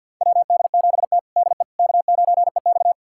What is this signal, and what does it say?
Morse code, heard as a keyed tone, about 51 wpm, 700 Hz. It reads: WD7T DE K9EX
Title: USA Callsign DE Callsign - 4 Characters - 50wpm